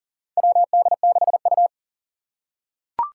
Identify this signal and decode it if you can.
Morse code, heard as a keyed tone, about 40 wpm, 700 Hz. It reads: WD6U
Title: USA Call Signs - 4 Characters - 3x Character Spacing - 40wpm